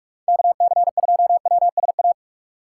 Morse code, heard as a keyed tone, about 46 wpm, 700 Hz. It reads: KC2WSA
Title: USA Call Signs - 6 Characters - 6x Character Spacing - 45wpm